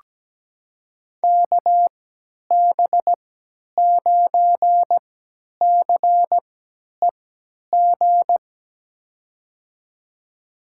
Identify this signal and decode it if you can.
Morse code, heard as a keyed tone, about 17 wpm, spaced as for 10 wpm, 700 Hz. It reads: KB9CEG
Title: USA Call Signs - 6 Characters - 3x Character Spacing - 17wpm